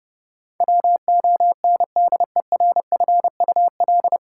Morse code, heard as a keyed tone, about 30 wpm, 700 Hz. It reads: WONDERFUL
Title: Sentences from Top 1100 Words - New Word Drill Alt - 30wpm